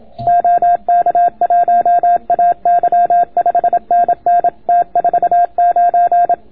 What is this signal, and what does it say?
Morse code, heard as a keyed tone, about 26 wpm, 690 Hz. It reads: OK1AY 5NN T49